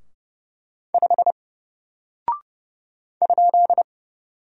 Morse code, heard as a keyed tone, about 31 wpm, 700 Hz. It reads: 5 ?
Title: Single Character (Letter:Number:Punctuation:Prosign) 30wpm